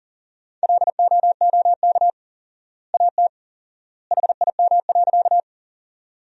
Morse code, heard as a keyed tone, about 40 wpm, 700 Hz. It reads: LOOK AT HIM.
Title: 4x Word-Spacing - Sentences from Top 100 Words - 40wpm